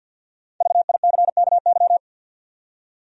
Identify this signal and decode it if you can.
Morse code, heard as a keyed tone, about 52 wpm, 700 Hz. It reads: VICKY
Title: USA Common Names - Max 5 Letters - 50wpm